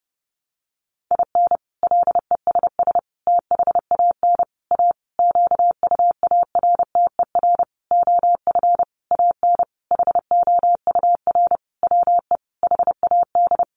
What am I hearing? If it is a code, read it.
Morse code, heard as a keyed tone, about 30 wpm, 700 Hz. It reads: IN LESS THAN A QUARTER OF AN HOUR WE HAD
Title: Sentences from Top 1300 Words 30wpm